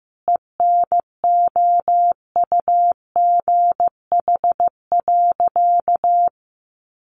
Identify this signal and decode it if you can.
Morse code, heard as a keyed tone, about 15 wpm, 700 Hz. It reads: ENOUGH.
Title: Sentences from Top 1000 Words 15wpm